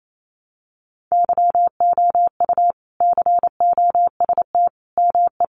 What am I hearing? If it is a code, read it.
Morse code, heard as a keyed tone, about 28 wpm, 700 Hz. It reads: YOU COST ME
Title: Sentences from Top 800 Words 28wpm